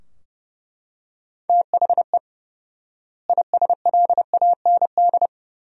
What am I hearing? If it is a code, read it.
Morse code, heard as a keyed tone, about 32 wpm, 700 Hz. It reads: THE ISLAND
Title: 4x Word-Spacing - Sentences from Top 500 Words - 30wpm